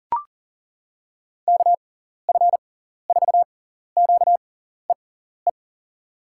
Morse code, heard as a keyed tone, about 40 wpm, spaced as for 14 wpm, 700 Hz. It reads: KF4QEE
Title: USA Call Signs - 6x Character Spacing - 40wpm